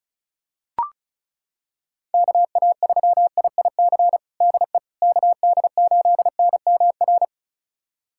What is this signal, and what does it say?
Morse code, heard as a keyed tone, about 34 wpm, 700 Hz. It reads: KA3IIC DE KD8NMR
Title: USA Callsign DE Callsign - 6 Characters - 35wpm